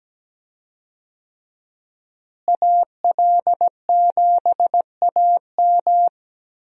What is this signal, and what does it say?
Morse code, heard as a keyed tone, about 17 wpm, 700 Hz. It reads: AL7AM
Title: USA Call Signs - 5 Characters - Repeated 2x - 17wpm